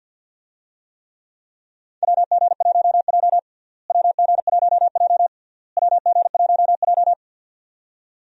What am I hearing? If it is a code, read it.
Morse code, heard as a keyed tone, about 47 wpm, 700 Hz. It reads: WG1J WG1J WG1J
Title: USA Call Signs - 4 Characters - Repeated 3x - 50wpm